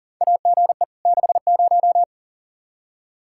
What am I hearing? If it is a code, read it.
Morse code, heard as a keyed tone, about 40 wpm, 690 Hz. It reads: AGE 60
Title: Practice QSO - Exchange 3 - More Info - 40wpm